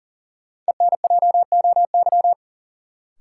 Morse code, heard as a keyed tone, about 40 wpm, 700 Hz. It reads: ENJOY
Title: Sentences from Top 1300 Words - New Word Drill Alt - 40wpm